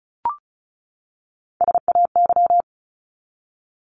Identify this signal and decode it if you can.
Morse code, heard as a keyed tone, about 34 wpm, 700 Hz. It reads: SAY
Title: ES:S Suffix - Top 100 Words with Base Words 35wpm